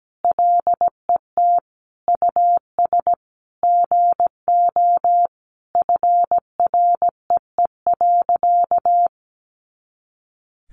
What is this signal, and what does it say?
Morse code, heard as a keyed tone, about 17 wpm, 700 Hz. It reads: LET US GO FREE.
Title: Sentences from Top 500 Words 17wpm